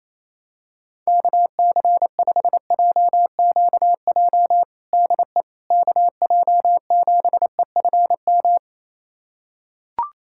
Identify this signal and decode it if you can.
Morse code, heard as a keyed tone, about 28 wpm, 700 Hz. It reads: KC5JQJ DE KJ7EFM
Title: USA Callsign DE Callsign - 6 Characters - 28wpm